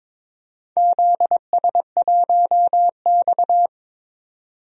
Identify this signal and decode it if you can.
Morse code, heard as a keyed tone, about 22 wpm, 700 Hz. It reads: ZS1X
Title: DXCC-100 Call Signs - 4 Characters - 22wpm